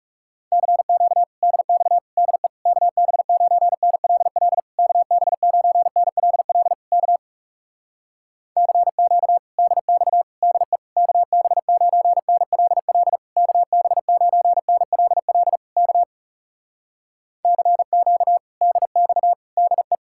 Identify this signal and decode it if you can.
Morse code, heard as a keyed tone, about 37 wpm, 700 Hz. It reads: CQ DX DE KB9NLL KB9NLL K CQ DX DE KB9NLL KB9NLL K CQ DX DE
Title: Speed-Racing Common QSO Phrases Encore - 30wpm